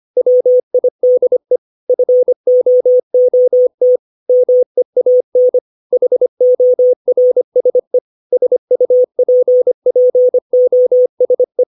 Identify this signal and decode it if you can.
Morse code, heard as a keyed tone, about 25 wpm, 500 Hz. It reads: WIDE FOOT MEAN HORSE SUPPOSE
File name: 632 words / 632 Words 25WPM